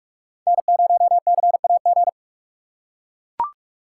Morse code, heard as a keyed tone, about 44 wpm, 700 Hz. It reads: N0CAG
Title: USA Call Signs - 5 Characters - 6x Character Spacing - 45wpm